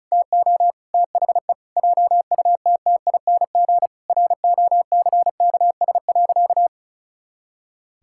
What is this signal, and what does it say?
Morse code, heard as a keyed tone, about 34 wpm, 700 Hz. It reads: TO THE JUTTING ROCKS.
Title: Pangram Challenge v2 - 35wpm